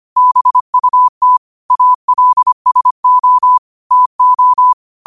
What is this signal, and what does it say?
Morse code, heard as a keyed tone, about 25 wpm, 1000 Hz. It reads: BUT ALSO TO
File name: qotdfiles / 25wpm